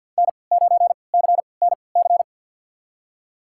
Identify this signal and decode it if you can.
Morse code, heard as a keyed tone, about 50 wpm, spaced as for 28 wpm, 700 Hz. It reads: N9CNC
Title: USA Call Signs - 5 Characters - 3x Character Spacing - 50wpm